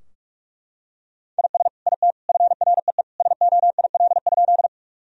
Morse code, heard as a keyed tone, about 43 wpm, 700 Hz. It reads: IS IT FREE SOIL?
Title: Sentences from Top 1000 Words - Review Part 4 - 45wpm